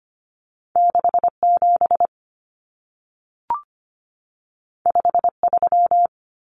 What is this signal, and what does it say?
Morse code, heard as a keyed tone, about 25 wpm, 700 Hz. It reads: 67 53
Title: Common Cut Numbers - 2-Digit 25wpm